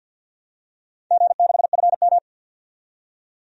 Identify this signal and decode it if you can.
Morse code, heard as a keyed tone, about 52 wpm, 700 Hz. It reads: G6FM
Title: DXCC-100 Call Signs - 4 Characters - 50wpm